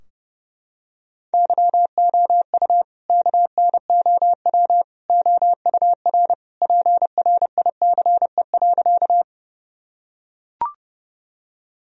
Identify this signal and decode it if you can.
Morse code, heard as a keyed tone, about 30 wpm, 700 Hz. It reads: YOU KNOW OUR PRICE.
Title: Sentences from Top 1000 Words - Review Part 4 - 30wpm